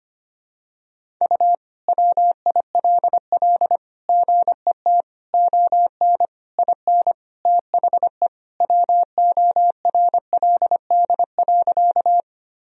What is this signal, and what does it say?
Morse code, heard as a keyed tone, about 25 wpm, 700 Hz. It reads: U WILL GET ON IN THE WORLD.
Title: Sentences from Top 200 Words and Abbreviations 25wpm